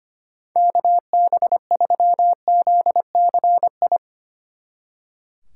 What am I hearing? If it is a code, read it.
Morse code, heard as a keyed tone, about 32 wpm, 700 Hz. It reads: KB3ZCI